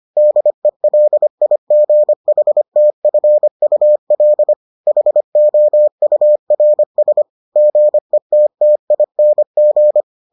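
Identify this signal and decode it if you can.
Morse code, heard as a keyed tone, about 25 wpm, 600 Hz. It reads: DELIGHTFUL HOURS GETTING
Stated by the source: Morse edition of the Foundations of Amateur Radio episodes.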